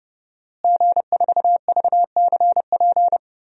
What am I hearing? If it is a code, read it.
Morse code, heard as a keyed tone, about 30 wpm, 700 Hz. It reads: G4VCP
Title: DXCC-100 Call Signs - 5 Characters - Repeated 2x - 30wpm